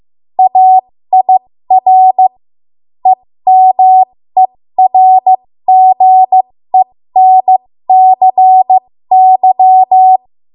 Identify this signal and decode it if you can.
Morse code, heard as a keyed tone, about 15 wpm, spaced as for 13 wpm, 750 Hz. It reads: AIR EMERGENCY
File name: Morse / 13WPM